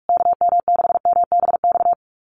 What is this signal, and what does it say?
Morse code, heard as a keyed tone, about 44 wpm, 700 Hz. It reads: KM6MBX
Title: USA Call Signs Repeated 3x 45wpm